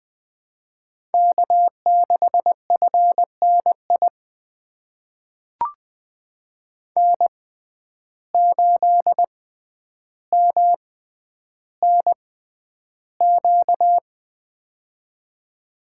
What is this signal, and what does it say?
Morse code, heard as a keyed tone, about 20 wpm, spaced as for 10 wpm, 700 Hz. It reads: K6FNI N8MNQ
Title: USA Call Signs - 5 Characters - 6x Character Spacing - 20wpm